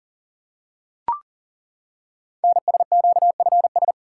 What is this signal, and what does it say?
Morse code, heard as a keyed tone, about 38 wpm, 690 Hz. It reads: NSQFS
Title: Code Groups - 5 Letters - 40wpm